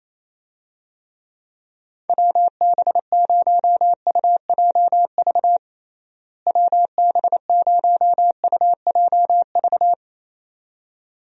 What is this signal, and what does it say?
Morse code, heard as a keyed tone, about 28 wpm, 700 Hz. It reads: WB0UJV WB0UJV
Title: USA Call Signs - 6 Characters - Repeated 2x - 28wpm